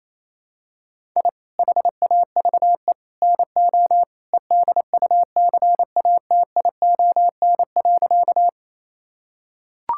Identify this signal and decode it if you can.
Morse code, heard as a keyed tone, about 27 wpm, 700 Hz. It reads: I HAVE NO EDUCATION.
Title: Sentences from Top 1000 Words - Review Part 6 - 28wpm